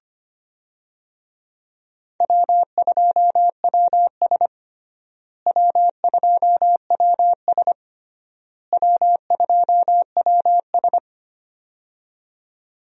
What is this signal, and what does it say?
Morse code, heard as a keyed tone, about 25 wpm, 700 Hz. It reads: W2WS W2WS W2WS
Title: USA Call Signs - 4 Characters - Repeated 3x - 25wpm